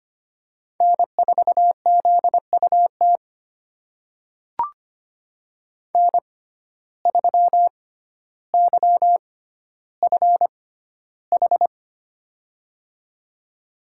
Morse code, heard as a keyed tone, about 25 wpm, spaced as for 13 wpm, 700 Hz. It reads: N4ZUT N3YFH
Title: USA Call Signs - 5 Characters - 6x Character Spacing - 25wpm